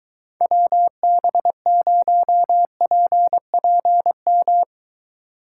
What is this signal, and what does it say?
Morse code, heard as a keyed tone, about 23 wpm, 700 Hz. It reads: WB0PPM